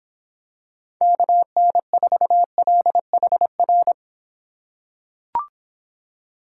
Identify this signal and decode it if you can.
Morse code, heard as a keyed tone, about 26 wpm, 700 Hz. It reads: KN4LHR